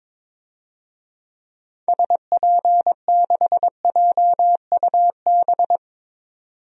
Morse code, heard as a keyed tone, about 22 wpm, 700 Hz. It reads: SP6JUB
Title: DXCC-100 Call Signs - 6 Characters - 22wpm